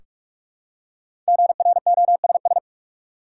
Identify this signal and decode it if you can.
Morse code, heard as a keyed tone, about 47 wpm, 700 Hz. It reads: GROSS
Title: Five Letter Words - 45wpm